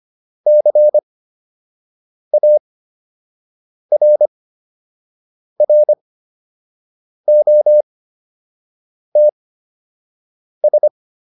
Morse code, heard as a keyed tone, about 25 wpm, spaced as for 6 wpm, 600 Hz. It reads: CARROTS